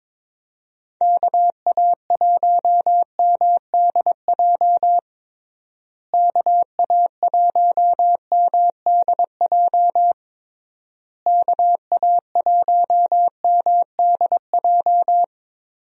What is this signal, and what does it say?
Morse code, heard as a keyed tone, about 22 wpm, 700 Hz. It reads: KA1MDJ KA1MDJ KA1MDJ
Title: USA Call Signs - 6 Characters - Repeated 3x - 22wpm